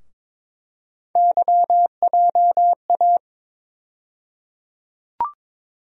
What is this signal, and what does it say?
Morse code, heard as a keyed tone, about 22 wpm, 700 Hz. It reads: YJA
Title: USA Call Sign Postfixes - 3 Letters 22wpm